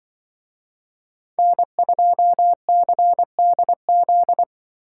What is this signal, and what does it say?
Morse code, heard as a keyed tone, about 25 wpm, 700 Hz. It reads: N2CDZ
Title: Speed Racing - USA 1x3 Call Signs 20wpm